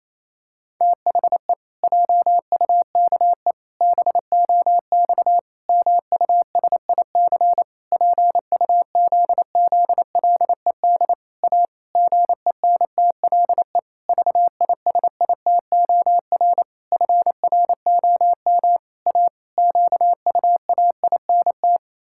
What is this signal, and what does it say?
Morse code, heard as a keyed tone, about 28 wpm, 700 Hz. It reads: THE JUKE BOX MUSIC PUZZLED A GENTLE VISITOR FROM A QUAINT
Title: Pangram Challenge v2 - 28wpm